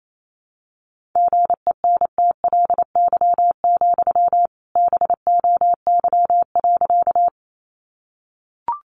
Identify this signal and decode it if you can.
Morse code, heard as a keyed tone, about 28 wpm, 700 Hz. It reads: GENTLY, BOY.
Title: Sentences from Top 1500 Words 28wpm